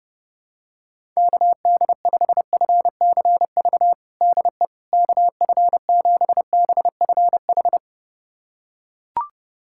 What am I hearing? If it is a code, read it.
Morse code, heard as a keyed tone, about 30 wpm, 700 Hz. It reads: KD5FCV DE KF7BFH
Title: USA Callsign DE Callsign 30wpm